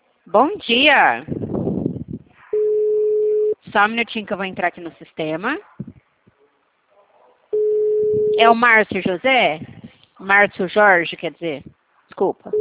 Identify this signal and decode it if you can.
Ringback tone (Europe, 425 Hz), with speech.
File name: out- -2048- - - .25444